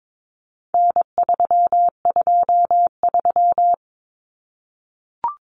Morse code, heard as a keyed tone, about 22 wpm, 700 Hz. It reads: N323
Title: Common Cut Numbers - 4-Digit 22wpm